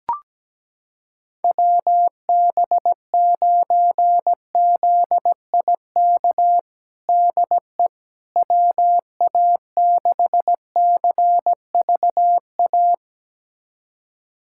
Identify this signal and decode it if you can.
Morse code, heard as a keyed tone, about 17 wpm, 700 Hz. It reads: WB9ZIK DE WA6CVA
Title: USA Callsign DE Callsign - 6 Characters - 17wpm